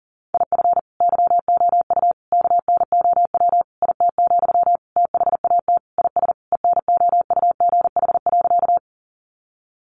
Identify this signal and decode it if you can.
Morse code, heard as a keyed tone, about 40 wpm, 700 Hz. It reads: IF YOU KNOW IT, THAT IS ENOUGH.
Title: Sentences from Top 400 Words 40wpm